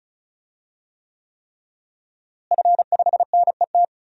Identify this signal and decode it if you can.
Morse code, heard as a keyed tone, about 36 wpm, 700 Hz. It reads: F5NET
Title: DXCC-100 Call Signs - 5 Characters - Repeated 3x - 35wpm